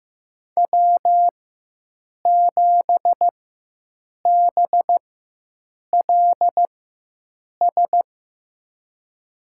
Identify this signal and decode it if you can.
Morse code, heard as a keyed tone, about 15 wpm, spaced as for 7 wpm, 700 Hz. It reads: W7BLS
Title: USA Call Signs - 5 Characters - 4x Character Spacing - 15wpm